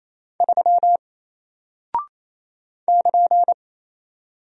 Morse code, heard as a keyed tone, about 28 wpm, 700 Hz. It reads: 3 <KN>
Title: Single Character (Letter:Number:Punctuation:Prosign) Alt 28wpm